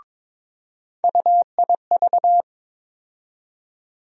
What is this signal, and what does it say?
Morse code, heard as a keyed tone, about 21 wpm, 700 Hz. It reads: UIV
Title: USA Call Sign Postfixes - 3 Letters 22wpm